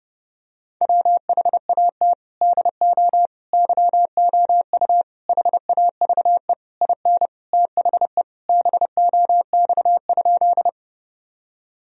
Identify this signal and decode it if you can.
Morse code, heard as a keyed tone, about 30 wpm, 700 Hz. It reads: WHAT DO YOU HAVE IN THE BOX?
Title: Sentences from Top 1000 Words - Review Part 2 - 30wpm